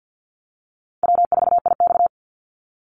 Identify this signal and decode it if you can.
Morse code, heard as a keyed tone, about 50 wpm, 700 Hz. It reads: W4IX
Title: Easy Call Signs Repeated 2x - 50wpm